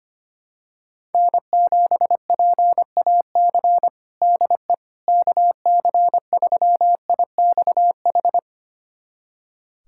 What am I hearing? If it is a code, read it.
Morse code, heard as a keyed tone, about 25 wpm, 700 Hz. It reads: N7PAC DE KC3IXH
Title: USA Callsign DE Callsign 25wpm